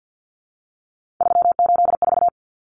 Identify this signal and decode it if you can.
Morse code, heard as a keyed tone, about 50 wpm, 700 Hz. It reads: 384
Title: Numbers - 3-Digit 50wpm